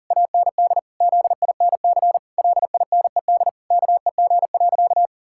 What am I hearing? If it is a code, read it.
Morse code, heard as a keyed tone, about 40 wpm, 700 Hz. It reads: AND ZINC LINED KEG.
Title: Pangram Challenge v2 - 40wpm